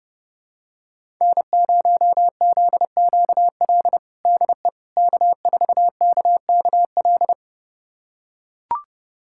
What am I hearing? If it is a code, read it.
Morse code, heard as a keyed tone, about 32 wpm, 700 Hz. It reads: N0ZQL DE K4KKL
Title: USA Callsign DE Callsign - 5 Characters - 30wpm